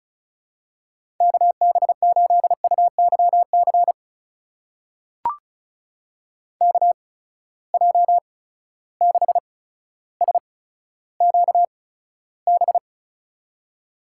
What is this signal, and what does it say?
Morse code, heard as a keyed tone, about 34 wpm, spaced as for 15 wpm, 700 Hz. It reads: KB8UYC KJ6SQB
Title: USA Call Signs - 6 Characters - 8x Character Spacing - 35wpm